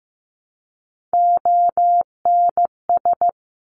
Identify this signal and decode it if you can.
Morse code, heard as a keyed tone, about 15 wpm, 700 Hz. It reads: ONS
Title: Top Three Letter Combinations in Words 15wpm